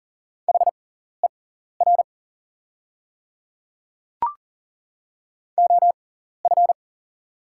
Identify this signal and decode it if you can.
Morse code, heard as a keyed tone, about 40 wpm, spaced as for 14 wpm, 690 Hz. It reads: HER OF
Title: 6x Character Spacing - Top 100 Words - 40wpm